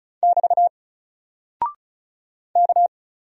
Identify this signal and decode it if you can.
Morse code, heard as a keyed tone, about 34 wpm, 700 Hz. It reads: <BT> K
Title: Single Character (Letter:Number:Punctuation:Prosign) 35wpm